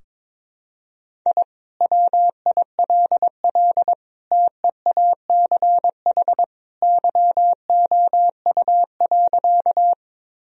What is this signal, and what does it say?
Morse code, heard as a keyed tone, about 21 wpm, 700 Hz. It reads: I WILL TEACH YOU.
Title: Sentences from Top 1000 Words - Review Part 2 - 22wpm